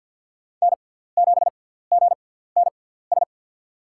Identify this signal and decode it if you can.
Morse code, heard as a keyed tone, about 50 wpm, spaced as for 17 wpm, 700 Hz. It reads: N7GNS
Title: USA Call Signs - 6x Character Spacing - 50wpm